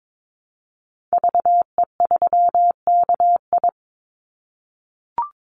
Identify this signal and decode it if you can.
Morse code, heard as a keyed tone, about 22 wpm, 700 Hz. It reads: VE3KI
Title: Call Signs - 22wpm